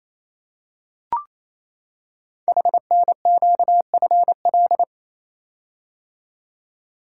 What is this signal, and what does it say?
Morse code, heard as a keyed tone, about 27 wpm, 700 Hz. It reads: HNQFL
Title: Code Groups - 5 Letters - 28wpm